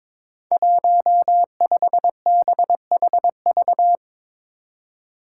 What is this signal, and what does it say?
Morse code, heard as a keyed tone, about 22 wpm, 700 Hz. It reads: 15BHV
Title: Five Challenging Letter-Number Combinations 22wpm